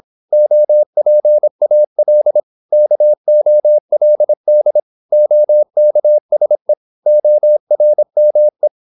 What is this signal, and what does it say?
Morse code, heard as a keyed tone, about 26 wpm, 600 Hz. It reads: OPAL KOLD OKSE ORME
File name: Korte ord DK 26wpm